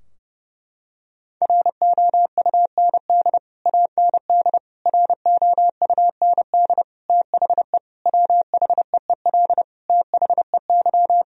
Morse code, heard as a keyed tone, about 30 wpm, 700 Hz. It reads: ROUND AND ROUND THE WHEEL THEY
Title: Sentences from Top 1000 Words - Review Part 3 - 30wpm